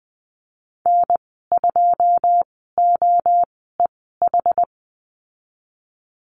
Morse code, heard as a keyed tone, about 20 wpm, spaced as for 14 wpm, 700 Hz. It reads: N2OEH
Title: 161 Lesson 26g - Letter g - Call Signs 20wpm